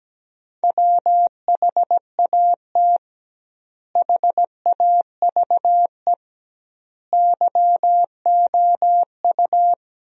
Morse code, heard as a keyed tone, about 17 wpm, 700 Hz. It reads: WHAT HAVE YOU
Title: 2x Word-Spacing - Sentences from Top 400 Words - 17wpm